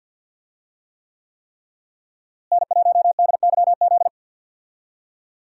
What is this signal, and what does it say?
Morse code, heard as a keyed tone, about 52 wpm, 700 Hz. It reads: N1DYZ
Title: USA Call Signs - 5 Characters - Repeated 3x - 50wpm